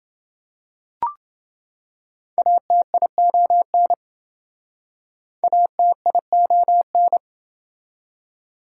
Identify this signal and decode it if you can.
Morse code, heard as a keyed tone, about 28 wpm, 700 Hz. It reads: ATION ATION